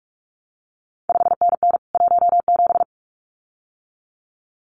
Morse code, heard as a keyed tone, about 46 wpm, 700 Hz. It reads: 5NN 17
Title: CQ WW DX Contest Exchange Encore - 45wpm